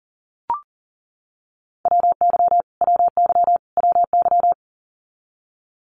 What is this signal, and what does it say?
Morse code, heard as a keyed tone, about 40 wpm, 700 Hz. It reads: WY WY WY
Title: US States Repeated 3x 40wpm